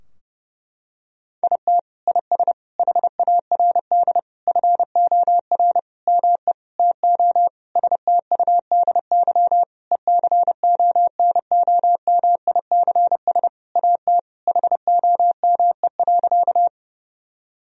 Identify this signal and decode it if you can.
Morse code, heard as a keyed tone, about 32 wpm, 700 Hz. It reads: IT IS HARD FOR ME TO STUDY ECONOMICS AT HOME.
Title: Sentences from Top 1000 Words - Review Part 3 - 30wpm